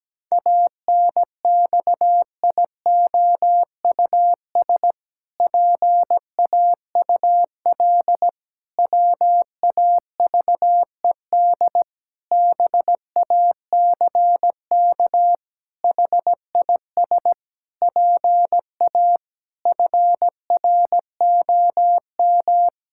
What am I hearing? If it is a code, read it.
Morse code, heard as a keyed tone, about 17 wpm, 700 Hz. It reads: ANXIOUS PAUL WAVED BACK HIS PA FROM
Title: Pangram Challenge v2 - 17wpm